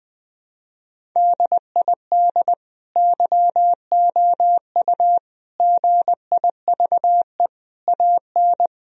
Morse code, heard as a keyed tone, about 20 wpm, 700 Hz. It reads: DID YOU GIVE AN
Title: Sentences from Top 1000 Words - Review Part 3 - 20wpm